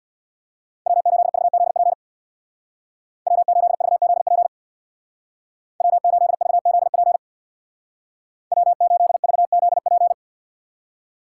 Morse code, heard as a keyed tone, about 61 wpm, 700 Hz. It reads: W8VZP W8VZP W8VZP W8VZP
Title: Speed Racing - USA 1x3 Call Signs 50wpm